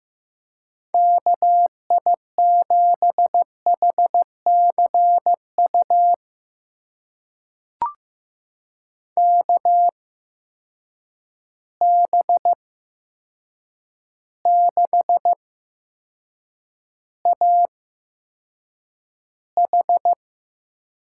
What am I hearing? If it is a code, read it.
Morse code, heard as a keyed tone, about 15 wpm, spaced as for 7 wpm, 700 Hz. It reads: KI7HCU KB6AH
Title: USA Call Signs - 6 Characters - 8x Character Spacing - 15wpm